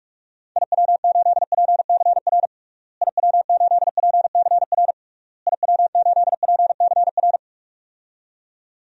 Morse code, heard as a keyed tone, about 41 wpm, 700 Hz. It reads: IW8PCR IW8PCR IW8PCR
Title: DXCC-100 Call Signs - 6 Characters - Repeated 3x - 45wpm